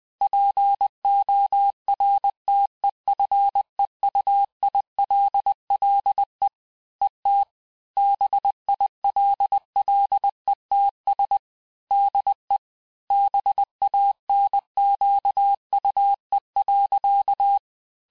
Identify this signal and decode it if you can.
Morse code, heard as a keyed tone, about 20 wpm, 800 Hz. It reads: PORTEFEUILLE ET BILLETS DE BANQUE.
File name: cours-du-mercredi / audios / cours-870-texte-CLAIR-vitesse-19-MPM